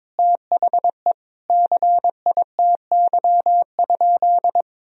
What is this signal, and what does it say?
Morse code, heard as a keyed tone, about 21 wpm, 700 Hz. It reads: THE CITY?
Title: Sentences from Top 1000 Words - Review Part 3 - 22wpm